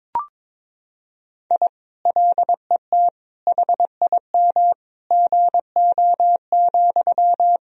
Morse code, heard as a keyed tone, about 22 wpm, 700 Hz. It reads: I LET HIM GO,
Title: Sentences from Top 1000 Words - Review Part 2 - 22wpm